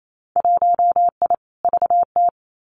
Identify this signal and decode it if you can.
Morse code, heard as a keyed tone, about 28 wpm, 700 Hz. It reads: 1I VT
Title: Winter Field Day Exchanges Encore 28wpm